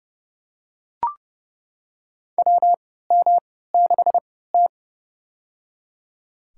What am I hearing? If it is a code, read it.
Morse code, heard as a keyed tone, about 30 wpm, spaced as for 17 wpm, 700 Hz. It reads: WM6T
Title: USA Call Signs - 4 Characters - 3x Character Spacing - 30wpm